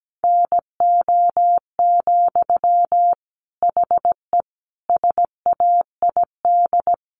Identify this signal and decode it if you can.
Morse code, heard as a keyed tone, about 17 wpm, 700 Hz. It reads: NO, HE SAID
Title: Sentences from Top 1100 Words 17wpm